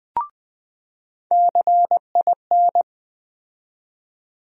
Morse code, heard as a keyed tone, about 21 wpm, 700 Hz. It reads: CIN
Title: USA Call Sign Postfixes - 3 Letters 20wpm